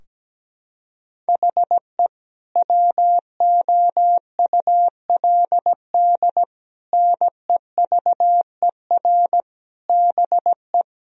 Morse code, heard as a keyed tone, about 17 wpm, 700 Hz. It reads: HE WOULD NEVER BE
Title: Sentences from Top 1000 Words - Review Part 3 - 17wpm